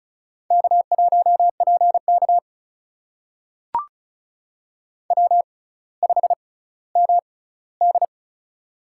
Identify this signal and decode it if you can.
Morse code, heard as a keyed tone, about 36 wpm, spaced as for 18 wpm, 700 Hz. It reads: K1PK W5MD
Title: USA Call Signs - 4 Characters - 6x Character Spacing - 35wpm